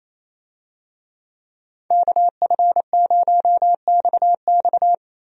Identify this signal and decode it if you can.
Morse code, heard as a keyed tone, about 28 wpm, 700 Hz. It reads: KF0XX
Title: USA Call Signs - 5 Characters - Repeated 2x - 28wpm